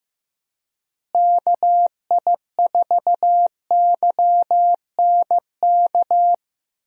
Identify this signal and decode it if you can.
Morse code, heard as a keyed tone, about 15 wpm, 700 Hz. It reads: KI4YNK
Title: USA Call Signs - 6 Characters - Repeated 3x - 15wpm